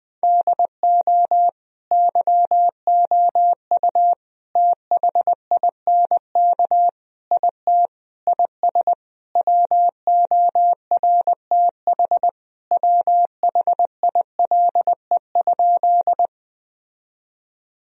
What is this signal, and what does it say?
Morse code, heard as a keyed tone, about 21 wpm, 700 Hz. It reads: DO YOU THINK IT IS WORTH WHILE?
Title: Sentences from Top 1000 Words - Review Part 1 - 20wpm